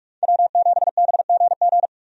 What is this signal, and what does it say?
Morse code, heard as a keyed tone, about 46 wpm, 700 Hz. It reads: W7BGG
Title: USA Call Signs - 5 Characters - 45wpm